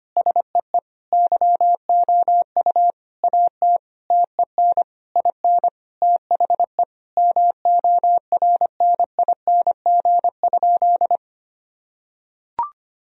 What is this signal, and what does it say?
Morse code, heard as a keyed tone, about 25 wpm, 700 Hz. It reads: SEE YOU AT TEN IN THE MORNING?
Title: Sentences from Top 1000 Words - Review Part 2 - 25wpm